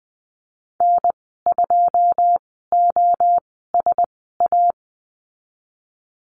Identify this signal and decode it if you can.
Morse code, heard as a keyed tone, about 20 wpm, spaced as for 15 wpm, 700 Hz. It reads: N2OSA
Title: 088 Lesson 15g - Number 5 - Call Signs 20wpm